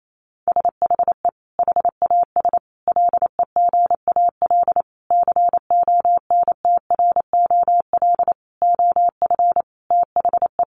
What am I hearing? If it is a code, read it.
Morse code, heard as a keyed tone, about 28 wpm, 700 Hz. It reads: SHE HAS LEGAL CONTROL OF THE
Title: Sentences from Top 1200 Words 28wpm